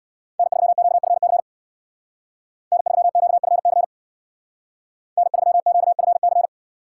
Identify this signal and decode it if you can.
Morse code, heard as a keyed tone, about 64 wpm, 700 Hz. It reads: N3QFC N3QFC N3QFC